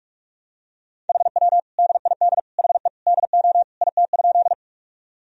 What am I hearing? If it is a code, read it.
Morse code, heard as a keyed tone, about 44 wpm, 700 Hz. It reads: HW DID HE DO IT?
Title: Sentences from Top 100 Words and Abbreviations 45wpm